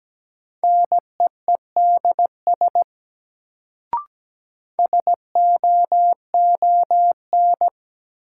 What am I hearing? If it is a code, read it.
Morse code, heard as a keyed tone, about 17 wpm, 700 Hz. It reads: NEEDS SOON
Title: Sets of 3 Words - New Word Drill - 17wpm